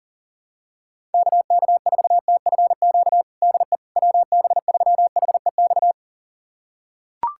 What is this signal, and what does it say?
Morse code, heard as a keyed tone, about 37 wpm, 700 Hz. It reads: KK4TFQ DE WB3HEX
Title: USA Callsign DE Callsign - 6 Characters - 40wpm